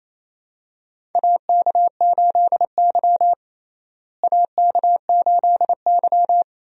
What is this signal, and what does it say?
Morse code, heard as a keyed tone, about 28 wpm, 700 Hz. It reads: AK8Y AK8Y
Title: USA Call Signs - 4 Characters - Repeated 2x - 28wpm